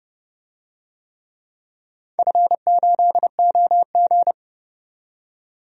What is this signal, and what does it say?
Morse code, heard as a keyed tone, about 31 wpm, 700 Hz. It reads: F8OG
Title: DXCC-100 Call Signs - 4 Characters - 30wpm